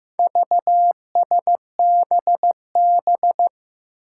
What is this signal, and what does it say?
Morse code, heard as a keyed tone, about 16 wpm, 700 Hz. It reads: VSBB
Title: Four Challenging Letter-Number Combinations 15wpm